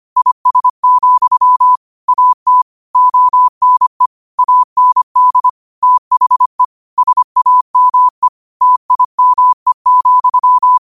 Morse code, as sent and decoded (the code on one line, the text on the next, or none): .. ... --..-- / .- - / --- -. / .- -. -.. / - .... / ... .- -- . / - .. -- . --..--
IS, AT ONE AND THE SAME TIME,